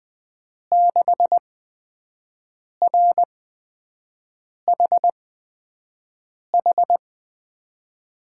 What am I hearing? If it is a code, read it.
Morse code, heard as a keyed tone, about 21 wpm, spaced as for 5 wpm, 700 Hz. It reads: 6RHH
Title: USA Call Signs - 6 Characters - 8x Character Spacing - 20wpm